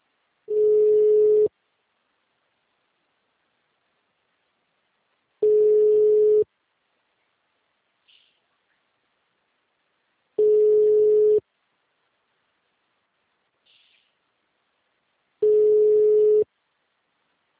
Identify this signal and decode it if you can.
Ringback tone (Europe, 425 Hz).
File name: out- -2005- - - .741